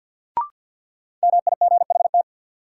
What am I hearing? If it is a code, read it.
Morse code, heard as a keyed tone, about 52 wpm, 700 Hz. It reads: MIGHT
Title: Sentences from Top 300 Words - New Word Drill - 50wpm